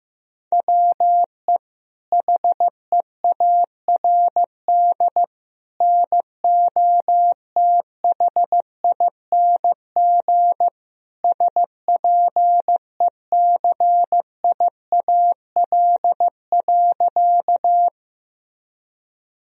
Morse code, heard as a keyed tone, about 15 wpm, 700 Hz. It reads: WE HEARD NOTHING SPECIAL.
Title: Sentences from Top 1000 Words - Review Part 3 - 15wpm